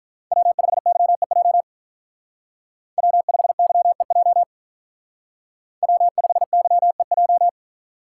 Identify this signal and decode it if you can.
Morse code, heard as a keyed tone, about 43 wpm, 700 Hz. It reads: W5YEJ W5YEJ W5YEJ
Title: Speed Racing - USA 1x3 Call Signs 35wpm